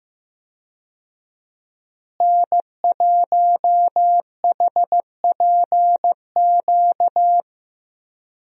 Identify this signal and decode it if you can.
Morse code, heard as a keyed tone, about 15 wpm, 700 Hz. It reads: N1HPQ
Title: USA Call Signs - 5 Characters - Repeated 2x - 15wpm